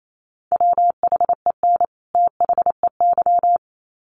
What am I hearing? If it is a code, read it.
Morse code, heard as a keyed tone, about 28 wpm, 700 Hz. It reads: WHEN THEY
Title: Sets of 2 Words 28wpm